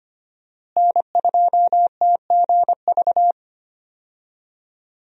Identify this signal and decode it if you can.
Morse code, heard as a keyed tone, about 25 wpm, 700 Hz. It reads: N2TGV
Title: USA Call Signs - 5 Characters - Repeated 3x - 25wpm